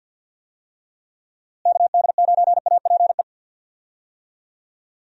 Morse code, heard as a keyed tone, about 48 wpm, 700 Hz. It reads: KD9APE
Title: USA Call Signs - 6 Characters - Repeated 2x - 50wpm